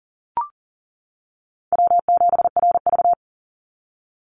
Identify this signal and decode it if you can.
Morse code, heard as a keyed tone, about 40 wpm, 700 Hz. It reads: W7RV
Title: NA SOTA Chaser Callsigns 40wpm